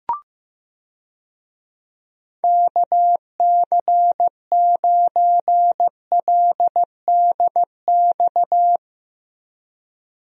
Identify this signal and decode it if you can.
Morse code, heard as a keyed tone, about 15 wpm, 700 Hz. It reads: KC9LDX
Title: USA Call Signs - 6 Characters - Repeated 3x - 15wpm